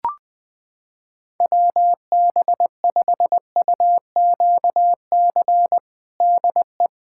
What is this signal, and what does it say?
Morse code, heard as a keyed tone, about 21 wpm, 700 Hz. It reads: WB5UQC DE
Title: USA Callsign DE Callsign - 6 Characters - 20wpm